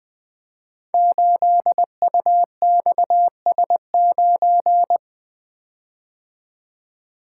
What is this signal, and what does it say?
Morse code, heard as a keyed tone, about 20 wpm, 700 Hz. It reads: 8UXS9
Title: Code Groups - 5 Letter:Numbers - 20wpm